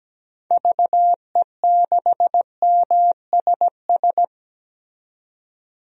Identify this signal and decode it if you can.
Morse code, heard as a keyed tone, about 17 wpm, 700 Hz. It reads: VE6MSS
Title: DXCC-100 Call Signs - 6 Characters - 17wpm